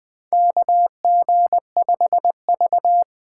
Morse code, heard as a keyed tone, about 20 wpm, 700 Hz. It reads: KG5V
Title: K1USN SST QSO Practice - 20wpm